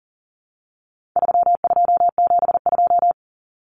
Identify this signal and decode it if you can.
Morse code, heard as a keyed tone, about 40 wpm, 700 Hz. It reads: 3272
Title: Common Cut Numbers - 4-Digit 40wpm